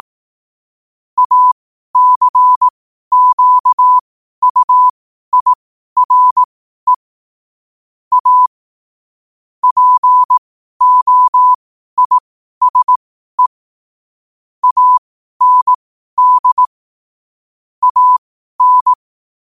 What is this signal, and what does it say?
Morse code, heard as a keyed tone, about 18 wpm, spaced as for 12 wpm, 1000 Hz. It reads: ACQUIRE A POISE AND AN
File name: qotdfiles / 12wpm